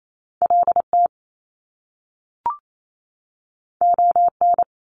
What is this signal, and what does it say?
Morse code, heard as a keyed tone, about 28 wpm, 700 Hz. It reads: LT ON
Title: Top Two Letter Combinations in Words 28wpm